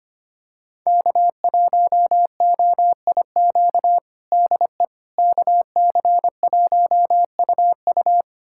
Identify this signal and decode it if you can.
Morse code, heard as a keyed tone, about 25 wpm, 700 Hz. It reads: K1OIQ DE KC1UU
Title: USA Callsign DE Callsign - 5 Characters - 25wpm